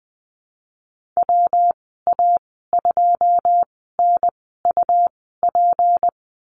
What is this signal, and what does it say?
Morse code, heard as a keyed tone, about 19 wpm, spaced as for 14 wpm, 700 Hz. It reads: WA2NUP
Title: 173 Lesson 28g - Number 9 - Call Signs 20wpm